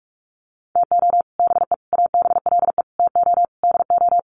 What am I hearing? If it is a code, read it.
Morse code, heard as a keyed tone, about 44 wpm, 700 Hz. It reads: TO BE ABLE TO DO
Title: Sets of 5 Words 45wpm